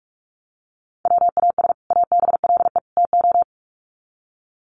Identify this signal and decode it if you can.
Morse code, heard as a keyed tone, about 44 wpm, 700 Hz. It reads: WAS ABLE TO
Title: Sets of 3 Words 45wpm